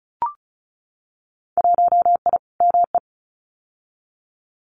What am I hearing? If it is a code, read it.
Morse code, heard as a keyed tone, about 34 wpm, 700 Hz. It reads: 1I ME
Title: Winter Field Day Exchanges 35wpm